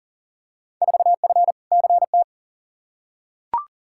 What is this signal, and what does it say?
Morse code, heard as a keyed tone, about 37 wpm, 700 Hz. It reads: 4F CT
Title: Field Day Exchanges Encore - 40wpm